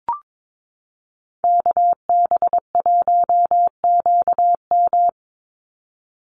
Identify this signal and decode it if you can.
Morse code, heard as a keyed tone, about 22 wpm, 700 Hz. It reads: KB1QM
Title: USA Call Signs 22wpm